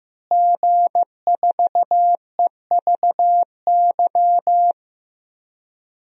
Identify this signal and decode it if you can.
Morse code, heard as a keyed tone, about 15 wpm, 700 Hz. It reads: G4EVY
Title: DXCC-100 Call Signs - 5 Characters - Repeated 3x - 15wpm